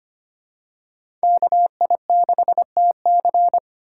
Morse code, heard as a keyed tone, about 25 wpm, 700 Hz. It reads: KI6TC
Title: USA Call Signs - 5 Characters - 2x Character Spacing - 25wpm